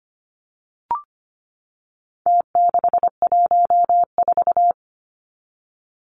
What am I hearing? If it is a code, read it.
Morse code, heard as a keyed tone, about 25 wpm, 700 Hz. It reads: T614
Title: Numbers - Contest Style - 4-Digit 25wpm